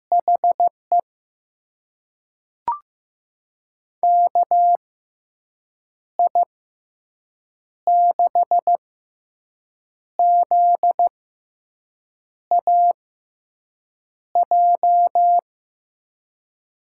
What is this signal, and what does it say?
Morse code, heard as a keyed tone, about 16 wpm, spaced as for 6 wpm, 700 Hz. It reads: HE KI6ZAJ
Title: USA Call Signs - 6 Characters - 6x Character Spacing - 15wpm